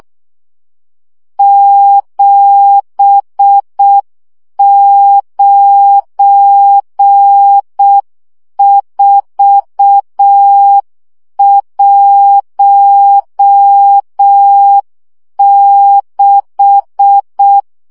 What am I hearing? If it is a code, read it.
Morse code, heard as a keyed tone, about 6 wpm, 800 Hz. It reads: 79416
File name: morze vizsga 2011-2 5